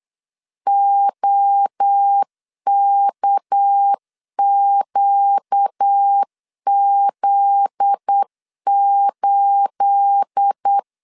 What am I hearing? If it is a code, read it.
Morse code, heard as a keyed tone, about 8 wpm, 800 Hz. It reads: OKQZ8
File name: CW / DET / 1 9 1500C S C 8wpm 5653